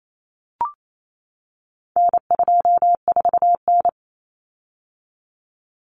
Morse code, heard as a keyed tone, about 28 wpm, 700 Hz. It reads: N24N
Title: Common Cut Numbers - 4-Digit 28wpm